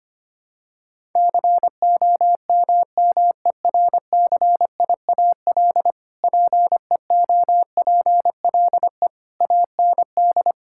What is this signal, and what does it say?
Morse code, heard as a keyed tone, about 25 wpm, 700 Hz. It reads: COMMERCIAL PEOPLE AND
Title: Sentences from Top 1000 Words - Review Part 5 - 25wpm